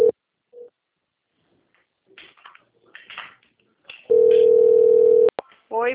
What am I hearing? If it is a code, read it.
Ringback tone (North America), 440 + 480 Hz, with speech.
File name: out- - - - - .38525